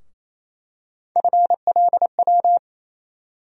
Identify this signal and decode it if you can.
Morse code, heard as a keyed tone, about 28 wpm, 700 Hz. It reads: FLW
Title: USA Call Sign Postfixes - 3 Letters 28wpm